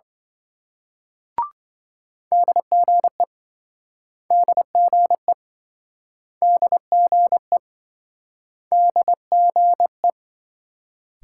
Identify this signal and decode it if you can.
Morse code, heard as a keyed tone, about 27 wpm, 700 Hz. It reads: DGE DGE DGE DGE